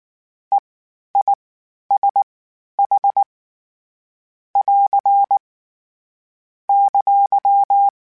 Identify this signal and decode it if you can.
Morse code, heard as a keyed tone, about 19 wpm, spaced as for 11 wpm, 800 Hz. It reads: EISH <AR> !
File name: Short Runs / Short Runs-mp3s / All DITs @18